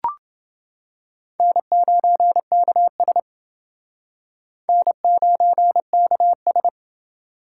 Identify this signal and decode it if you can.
Morse code, heard as a keyed tone, about 30 wpm, 700 Hz. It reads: N9KS N9KS